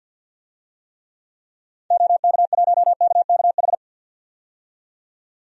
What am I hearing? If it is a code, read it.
Morse code, heard as a keyed tone, about 48 wpm, 700 Hz. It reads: OK1KKH
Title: DXCC-100 Call Signs - 6 Characters - 50wpm